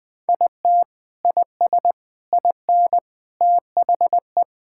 Morse code, heard as a keyed tone, about 20 wpm, 700 Hz. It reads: IT IS IN THE
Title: Sentences with 7-Word Phrases - Spoken Beginning - 20wpm